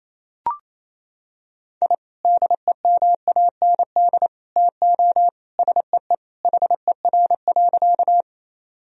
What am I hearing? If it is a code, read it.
Morse code, heard as a keyed tone, about 27 wpm, 700 Hz. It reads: I DEMAND TO SEE HER.
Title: Sentences from Top 1000 Words - Review Part 4 - 28wpm